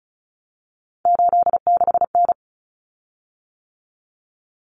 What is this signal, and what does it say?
Morse code, heard as a keyed tone, about 34 wpm, 700 Hz. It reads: 86N
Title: Common Cut Numbers - 3-Digit 35wpm